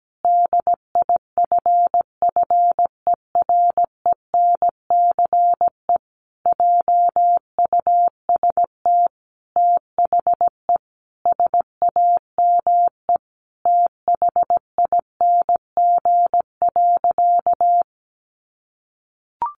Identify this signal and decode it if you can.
Morse code, heard as a keyed tone, about 17 wpm, 700 Hz. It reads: DIFFERENCE JUST THE SAME THING.
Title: Sentences from Top 800 Words 17wpm